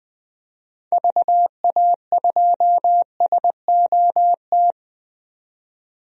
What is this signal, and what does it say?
Morse code, heard as a keyed tone, about 20 wpm, 700 Hz. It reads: VA2SOT
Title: DXCC-100 Call Signs - 6 Characters - 20wpm